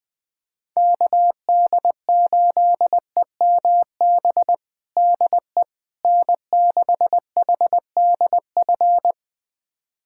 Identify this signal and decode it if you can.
Morse code, heard as a keyed tone, about 21 wpm, 700 Hz. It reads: KD8EMB DE N6HDF
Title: USA Callsign DE Callsign 20wpm